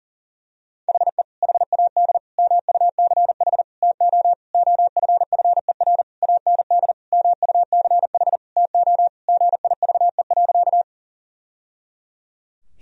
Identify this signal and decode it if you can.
Morse code, heard as a keyed tone, about 40 wpm, 700 Hz. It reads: HE HAD MUCH TO OFFER AND MUCH TO GIVE.
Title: Sentences from Top 1000 Words - Review Part 4 - 40wpm